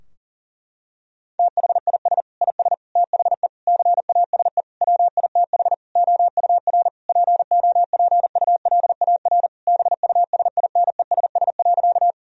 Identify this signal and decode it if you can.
Morse code, heard as a keyed tone, about 40 wpm, 700 Hz. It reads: THIS IS THE CASE WITH OUR POPULAR BUSINESS.
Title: Sentences from Top 1000 Words - Review Part 3 - 40wpm